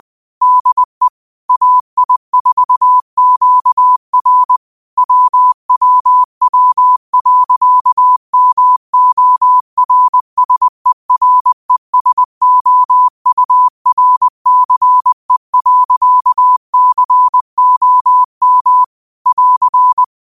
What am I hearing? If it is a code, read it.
Morse code, heard as a keyed tone, about 20 wpm, 1000 Hz. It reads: DE AI4QR WWW.MORSERESOURCE.COM <AR>
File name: 20wpm